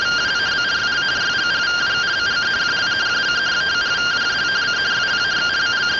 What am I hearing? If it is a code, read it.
Two-tone teleprinter signal, about 170 Hz shift.